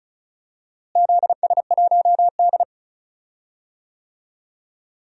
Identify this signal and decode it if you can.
Morse code, heard as a keyed tone, about 36 wpm, 700 Hz. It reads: ZS1D
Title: DXCC-100 Call Signs - 4 Characters - Repeated 2x - 35wpm